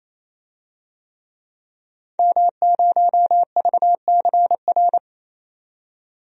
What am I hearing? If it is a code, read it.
Morse code, heard as a keyed tone, about 28 wpm, 690 Hz. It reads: M0VCR
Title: DXCC-100 Call Signs - 5 Characters - 28wpm